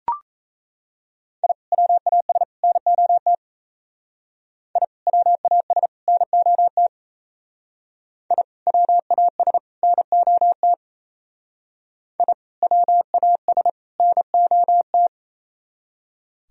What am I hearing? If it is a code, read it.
Morse code, heard as a keyed tone, about 36 wpm, 700 Hz. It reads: I WAS NOT I WAS NOT I WAS NOT I WAS NOT